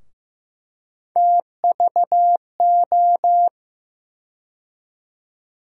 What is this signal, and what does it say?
Morse code, heard as a keyed tone, about 15 wpm, 700 Hz. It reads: TVO
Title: USA Call Sign Postfixes - 3 Letters 15wpm